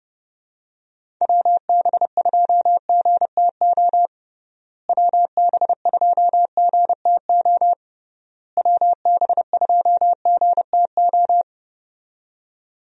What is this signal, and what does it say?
Morse code, heard as a keyed tone, about 30 wpm, 700 Hz. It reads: WB2GTO WB2GTO WB2GTO
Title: USA Call Signs - 6 Characters - Repeated 3x - 30wpm